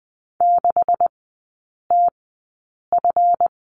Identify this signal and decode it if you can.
Morse code, heard as a keyed tone, about 20 wpm, spaced as for 8 wpm, 700 Hz. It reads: 6TF
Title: 145 Lesson 24e - Letter y - Three Characters 20wpm